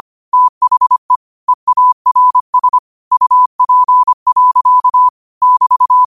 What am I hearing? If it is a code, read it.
Morse code, heard as a keyed tone, about 25 wpm, 1000 Hz. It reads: THE EARS UP. <BT>
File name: qotdfiles / 25wpm